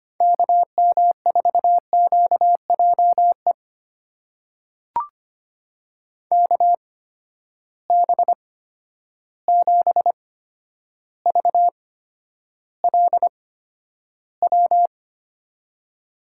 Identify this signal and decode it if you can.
Morse code, heard as a keyed tone, about 25 wpm, spaced as for 11 wpm, 700 Hz. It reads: KM4QJE KB7VLW
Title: USA Call Signs - 6 Characters - 8x Character Spacing - 25wpm